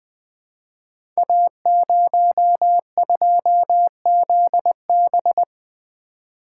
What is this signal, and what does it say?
Morse code, heard as a keyed tone, about 20 wpm, 700 Hz. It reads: A02ZB
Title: Code Groups - 5 Letter:Numbers - 20wpm